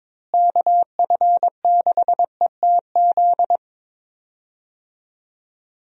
Morse code, heard as a keyed tone, about 21 wpm, 700 Hz. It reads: KF6ETZ
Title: USA Call Signs - 6 Characters - Repeated 3x - 22wpm